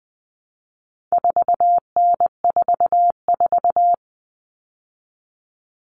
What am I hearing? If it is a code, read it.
Morse code, heard as a keyed tone, about 20 wpm, 700 Hz. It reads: 4N44
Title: All Cut Numbers - 4-Digit 20wpm